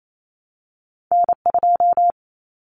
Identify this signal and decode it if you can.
Morse code, heard as a keyed tone, about 28 wpm, 700 Hz. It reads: N 2
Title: Common Cut Numbers - 2-Digit 28wpm